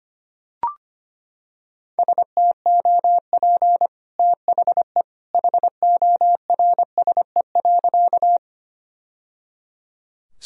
Morse code, heard as a keyed tone, about 25 wpm, 700 Hz. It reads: STOP THE HORSE.
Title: Sentences from Top 1000 Words - Review Part 1 - 25wpm